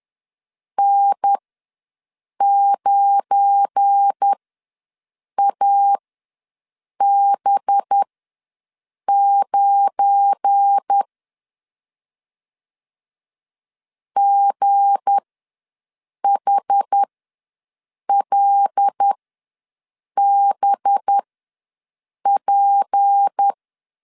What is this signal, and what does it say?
Morse code, heard as a keyed tone, about 10 wpm, spaced as for 6 wpm, 800 Hz. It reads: N9AB9 GHLBP
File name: LEZIONI / 7 GF9Z / 7 GF9 ws 10wpm 4034